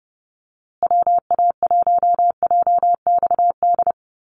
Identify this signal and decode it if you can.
Morse code, heard as a keyed tone, about 30 wpm, 700 Hz. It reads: WA1JXD
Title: USA Call Signs Repeated 2x 30wpm